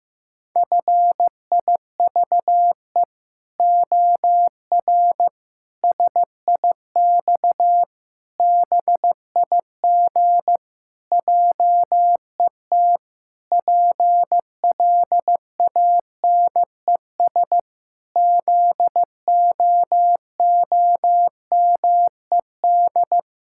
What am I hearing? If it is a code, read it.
Morse code, heard as a keyed tone, about 16 wpm, 700 Hz. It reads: FIVE OR SIX BIG JET PLANES ZOOMED
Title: Pangram Challenge v2 - 15wpm